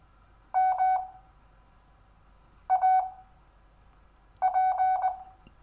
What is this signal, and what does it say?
Morse code, heard as a keyed tone, about 20 wpm, spaced as for 5 wpm, 750 Hz. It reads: MAP
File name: cw / Lesson4 REVIEW